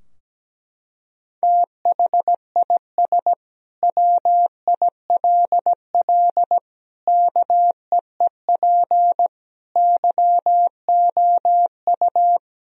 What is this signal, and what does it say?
Morse code, heard as a keyed tone, about 17 wpm, 700 Hz. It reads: THIS WILL KEEP YOU
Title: Sentences from Top 1000 Words - Review Part 2 - 17wpm